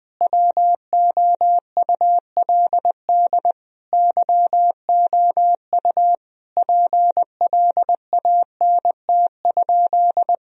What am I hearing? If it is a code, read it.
Morse code, heard as a keyed tone, about 20 wpm, 700 Hz. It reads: WOULD YOU PLANT?
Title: Sentences from Top 1000 Words - Review Part 1 - 20wpm